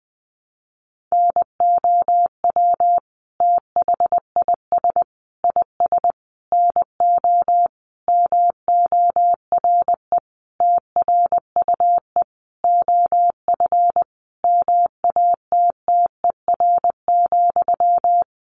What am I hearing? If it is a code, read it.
Morse code, heard as a keyed tone, about 20 wpm, 700 Hz. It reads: NOW THIS IS NO MORE TRUE OF MATTER,
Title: Sentences from Top 1400 Words 20wpm